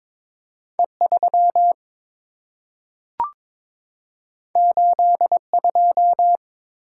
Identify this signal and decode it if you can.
Morse code, heard as a keyed tone, about 22 wpm, 700 Hz. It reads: E3 82
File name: practice / 22wpm / Two Letter-Number Combinations - 22wpm